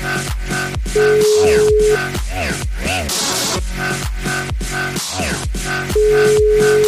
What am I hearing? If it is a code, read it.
Ringback tone (Europe, 425 Hz), with noise.